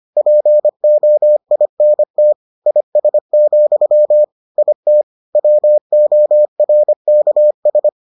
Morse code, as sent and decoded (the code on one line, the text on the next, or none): .--. --- .. -. - / .. ... --..-- / .. - / .-- --- .-. -.- ...
POINT IS, IT WORKS